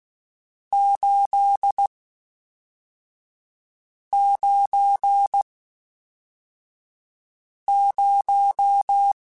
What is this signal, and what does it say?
Morse code, heard as a keyed tone, about 16 wpm, 800 Hz. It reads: 8 9 0